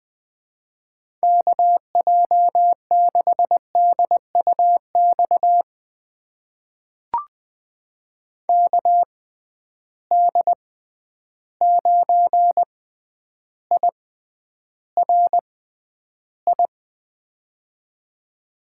Morse code, heard as a keyed tone, about 20 wpm, spaced as for 10 wpm, 700 Hz. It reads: KJ6DUX KD9IRI
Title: USA Call Signs - 6 Characters - 6x Character Spacing - 20wpm